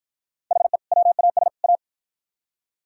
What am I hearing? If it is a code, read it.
Morse code, heard as a keyed tone, about 52 wpm, 700 Hz. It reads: HE WAS A